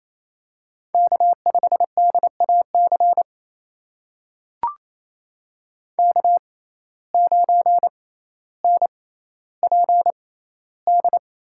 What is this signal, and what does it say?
Morse code, heard as a keyed tone, about 28 wpm, spaced as for 14 wpm, 700 Hz. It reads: K5DAC K9NPD
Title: USA Call Signs - 5 Characters - 6x Character Spacing - 28wpm